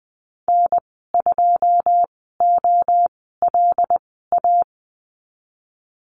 Morse code, heard as a keyed tone, about 20 wpm, spaced as for 15 wpm, 700 Hz. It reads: N2OLA
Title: 186 Lesson 30g - Letter b - Call Signs 20wpm